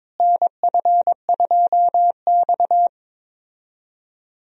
Morse code, heard as a keyed tone, about 22 wpm, 700 Hz. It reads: NF2X
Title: USA Call Signs - 4 Characters - Repeated 2x - 22wpm